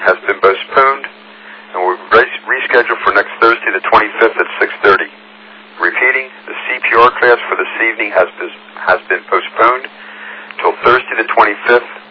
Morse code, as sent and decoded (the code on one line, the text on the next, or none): none